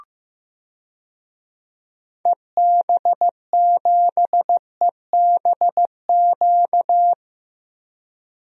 Morse code, heard as a keyed tone, about 14 wpm, 700 Hz. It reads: EB7EBQ
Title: DXCC-100 Call Signs - 6 Characters - Repeated 3x - 15wpm